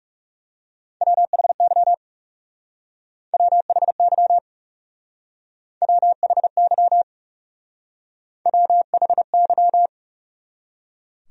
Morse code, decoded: WHY WHY WHY WHY

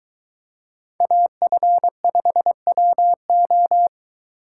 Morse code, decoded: AF5WO

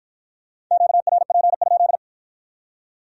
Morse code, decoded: QRP?

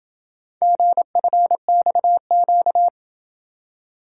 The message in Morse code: GFXQ

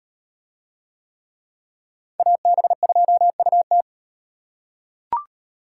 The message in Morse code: AB2UT

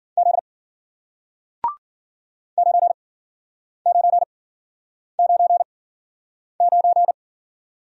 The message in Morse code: Z 9999

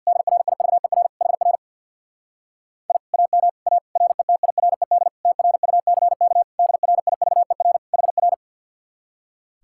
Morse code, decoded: DRIVER HR I AM A RETIRED TRUCK DRIVER HR

